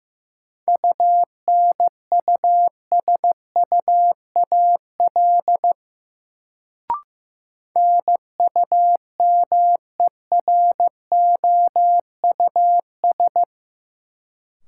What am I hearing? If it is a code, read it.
Morse code, heard as a keyed tone, about 15 wpm, 700 Hz. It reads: UNUSUAL NUMEROUS
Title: Sentences from Top 1400 Words - New Word Drill Alt - 15wpm